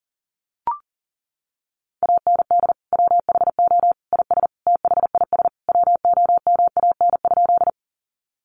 Morse code, heard as a keyed tone, about 40 wpm, 700 Hz. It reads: AND WHO IS THIS WOMAN?
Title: Sentences from Top 1100 Words 40wpm